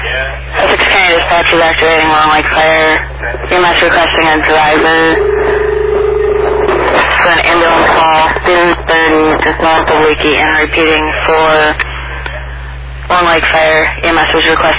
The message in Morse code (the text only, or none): none